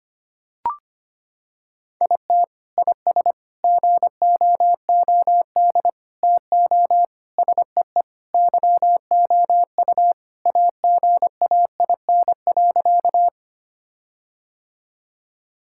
Morse code, heard as a keyed tone, about 25 wpm, 700 Hz. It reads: IT IS GOOD TO SEE YOU AGAIN.
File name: practice / 25wpm / Sentences with 4-Word Phrases - 25wpm